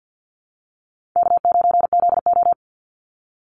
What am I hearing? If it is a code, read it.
Morse code, heard as a keyed tone, about 50 wpm, 700 Hz. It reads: K9ZO
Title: Easy Call Signs Repeated 2x - 50wpm